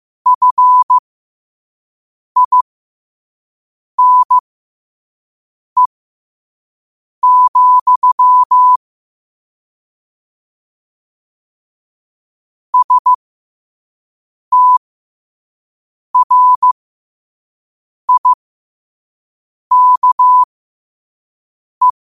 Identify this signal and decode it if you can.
Morse code, heard as a keyed tone, about 15 wpm, spaced as for 5 wpm, 1000 Hz. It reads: FINE, STRIKE
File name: qotdfiles / 5wpm